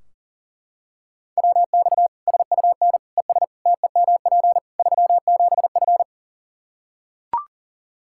Morse code, heard as a keyed tone, about 40 wpm, 700 Hz. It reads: WX SUN ES TEMP 37F
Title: Practice QSO - Exchange 2 - Info - 40wpm